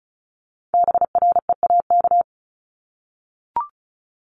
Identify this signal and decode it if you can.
Morse code, heard as a keyed tone, about 36 wpm, 700 Hz. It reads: BREAK
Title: Multi-letter Phonemes 35wpm